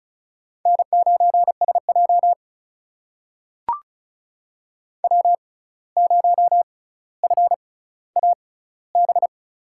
Morse code, heard as a keyed tone, about 36 wpm, spaced as for 17 wpm, 700 Hz. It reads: N9SJ W0FAB
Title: USA Call Signs - 6x Character Spacing - 35wpm